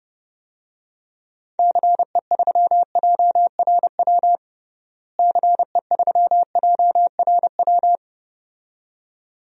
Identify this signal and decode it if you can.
Morse code, heard as a keyed tone, about 30 wpm, 700 Hz. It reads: CE3JRW CE3JRW
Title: DXCC-100 Call Signs - 6 Characters - Repeated 2x - 30wpm